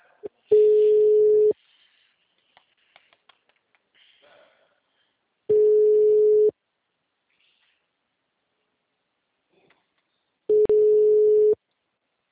Ringback tone (Europe, 425 Hz).
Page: Rio Claro